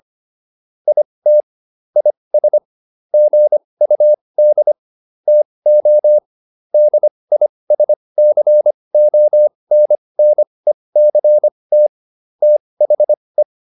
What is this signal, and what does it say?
Morse code, heard as a keyed tone, about 25 wpm, spaced as for 20 wpm, 600 Hz. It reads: IT IS GUD TO DISCONNECT THE